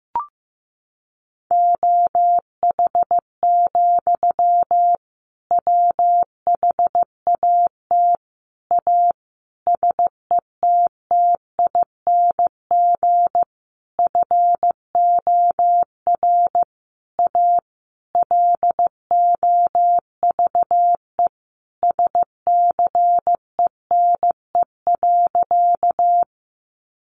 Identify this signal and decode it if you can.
Morse code, heard as a keyed tone, about 15 wpm, 700 Hz. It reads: OH, WHAT A SETTING FOR A LOVE SCENE.
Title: Sentences from Top 1300 Words 15wpm